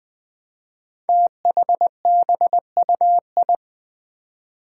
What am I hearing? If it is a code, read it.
Morse code, heard as a keyed tone, about 20 wpm, 700 Hz. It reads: THBUI
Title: Five Challenging Letter-Number Combinations 20wpm